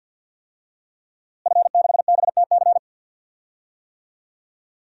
Morse code, heard as a keyed tone, about 52 wpm, 700 Hz. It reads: F6BTC
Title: DXCC-100 Call Signs - 5 Characters - Repeated 2x - 50wpm